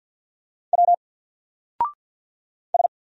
Morse code, heard as a keyed tone, about 48 wpm, 700 Hz. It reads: W S
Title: Single Character (Letter:Number:Punctuation:Prosign) Alt 50wpm